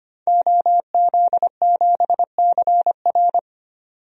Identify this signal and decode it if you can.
Morse code, heard as a keyed tone, about 25 wpm, 700 Hz. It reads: OZ7CR
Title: DXCC-100 Call Signs - 5 Characters - 25wpm